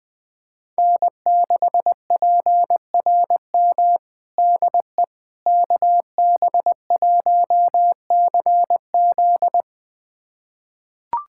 Morse code, heard as a keyed tone, about 20 wpm, 700 Hz. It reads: N6PRM DE KB1CZ
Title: USA Callsign DE Callsign - 5 Characters - 20wpm